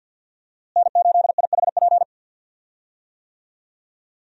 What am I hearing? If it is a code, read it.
Morse code, heard as a keyed tone, about 50 wpm, 700 Hz. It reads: N8IHP
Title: USA Call Signs - 5 Characters - Repeated 3x - 50wpm